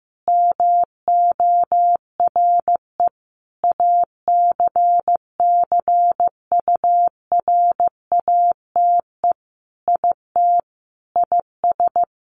Morse code, heard as a keyed tone, about 15 wpm, 700 Hz. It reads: MORE ACCURATE IT IS
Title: Sentences from Top 1200 Words 15wpm